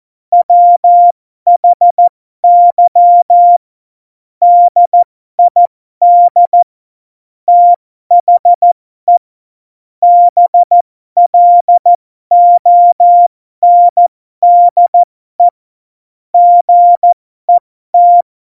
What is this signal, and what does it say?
Morse code, heard as a keyed tone, about 14 wpm, spaced as for 12 wpm, 700 Hz. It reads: WHY DID THE BLONDE GET
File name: Morsekurs / Witze